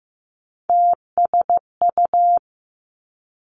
Morse code, heard as a keyed tone, about 15 wpm, 700 Hz. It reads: TSU
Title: Three Challenging Letter-Number Combinations 15wpm